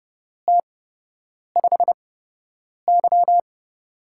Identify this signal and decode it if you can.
Morse code, heard as a keyed tone, about 30 wpm, spaced as for 8 wpm, 700 Hz. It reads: T5Y
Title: USA Call Signs - 8x Character Spacing - 30wpm